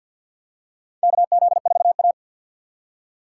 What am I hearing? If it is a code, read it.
Morse code, heard as a keyed tone, about 52 wpm, 700 Hz. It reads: KZ4A
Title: K1USN SST QSO Practice - 50wpm